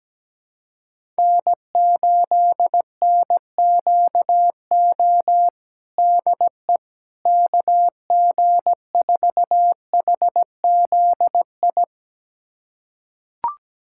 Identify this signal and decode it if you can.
Morse code, heard as a keyed tone, about 17 wpm, 700 Hz. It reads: N8NQO DE KG4HZI
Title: USA Callsign DE Callsign 17wpm